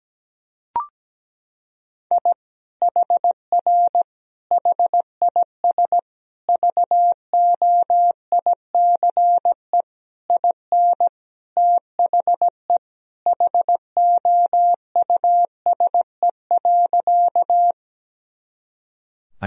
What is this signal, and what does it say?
Morse code, heard as a keyed tone, about 17 wpm, 700 Hz. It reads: I HR HIS VOICE IN THE HOUSE.
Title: Sentences from Top 500 Words and Abbreviations 17wpm